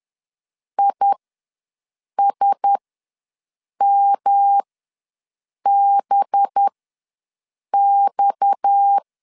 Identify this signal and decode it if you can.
Morse code, heard as a keyed tone, about 10 wpm, spaced as for 6 wpm, 800 Hz. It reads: ISMBX
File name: LEZIONI / 7 GF9Z / 7 G ws 10wpm 3843